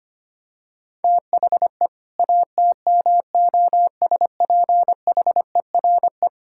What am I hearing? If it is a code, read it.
Morse code, heard as a keyed tone, about 25 wpm, 700 Hz. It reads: THE ATMOSPHERE
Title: Sentences from Top 1600 Words 25wpm